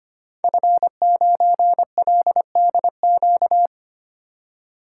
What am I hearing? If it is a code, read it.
Morse code, heard as a keyed tone, about 25 wpm, 700 Hz. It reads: F9LDQ
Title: Code Groups - 5 Letter:Numbers - 25wpm